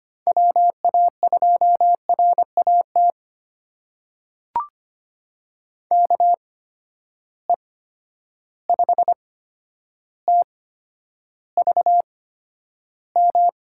Morse code, heard as a keyed tone, about 25 wpm, spaced as for 11 wpm, 700 Hz. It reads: WA2RAT KE5TVM
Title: USA Call Signs - 6 Characters - 8x Character Spacing - 25wpm